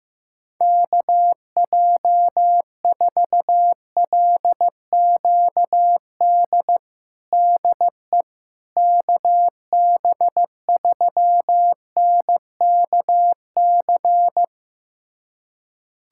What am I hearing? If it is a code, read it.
Morse code, heard as a keyed tone, about 15 wpm, 700 Hz. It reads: KJ4LQD DE KB3NKC
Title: USA Callsign DE Callsign - 6 Characters - 15wpm